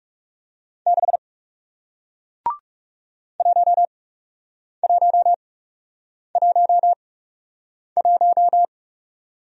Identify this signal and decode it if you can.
Morse code, heard as a keyed tone, about 39 wpm, spaced as for 8 wpm, 700 Hz. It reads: 6 1111